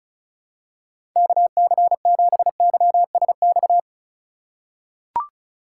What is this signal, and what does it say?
Morse code, heard as a keyed tone, about 34 wpm, 700 Hz. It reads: KC7YSX
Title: USA Call Signs - 6 Characters - 3x Character Spacing - 35wpm